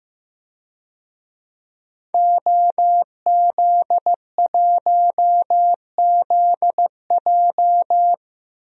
Morse code, heard as a keyed tone, about 15 wpm, 700 Hz. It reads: OZ1ZJ
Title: DXCC-100 Call Signs - 5 Characters - Repeated 3x - 15wpm